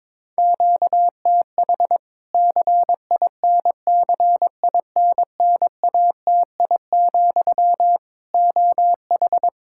Morse code, heard as a keyed tone, about 22 wpm, 700 Hz. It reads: QTH CINCINNATI, OH
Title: Practice QSO - Exchange 1 - Beginning - 22wpm